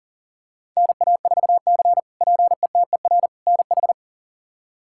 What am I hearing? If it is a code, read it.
Morse code, heard as a keyed tone, about 37 wpm, 700 Hz. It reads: NA4C PETER NH
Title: K1USN SST QSO Practice - 40wpm